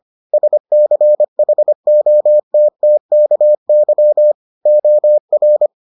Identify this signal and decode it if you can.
Morse code, heard as a keyed tone, about 25 wpm, 600 Hz. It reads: SCHOTTKY OR